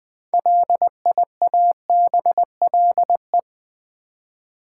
Morse code, heard as a keyed tone, about 20 wpm, 700 Hz. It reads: LIABLE
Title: 6 Letter Words - First 3 Letters Given - 20wpm